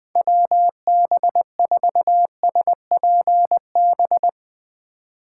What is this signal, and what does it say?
Morse code, heard as a keyed tone, about 20 wpm, 700 Hz. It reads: WB4SPB
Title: K1USN SST QSO Practice - 20wpm